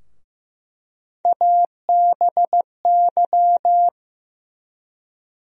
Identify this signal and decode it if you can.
Morse code, heard as a keyed tone, about 15 wpm, 700 Hz. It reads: ABY
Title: USA Call Sign Postfixes - 3 Letters 15wpm